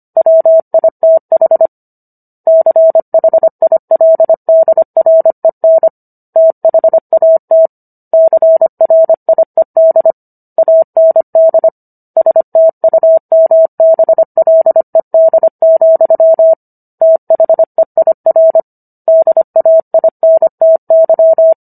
Morse code, heard as a keyed tone, about 25 wpm, 650 Hz. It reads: WITH CHILDREN THAT CRIED AND STUMBLED, THEIR DAINTY